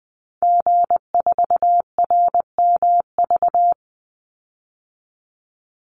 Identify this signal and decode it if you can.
Morse code, heard as a keyed tone, about 20 wpm, 700 Hz. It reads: G4RMV
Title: Call Signs Repeated 3x - 20wpm